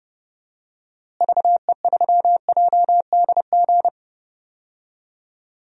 Morse code, heard as a keyed tone, about 30 wpm, 700 Hz. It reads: VE3JDG
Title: DXCC-100 Call Signs - 6 Characters - Repeated 3x - 30wpm